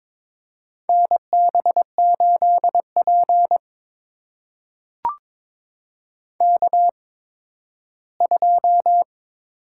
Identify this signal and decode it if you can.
Morse code, heard as a keyed tone, about 22 wpm, spaced as for 13 wpm, 700 Hz. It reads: NB8P K2
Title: USA Call Signs - 4 Characters - 8x Character Spacing - 22wpm